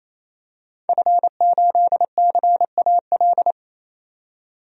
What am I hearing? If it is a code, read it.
Morse code, heard as a keyed tone, about 28 wpm, 700 Hz. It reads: F8CAL
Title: DXCC-100 Call Signs - 5 Characters - 28wpm